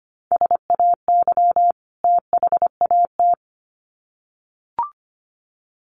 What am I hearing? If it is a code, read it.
Morse code, heard as a keyed tone, about 25 wpm, 700 Hz. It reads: SAY THAT
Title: Sets of 2 Words 25wpm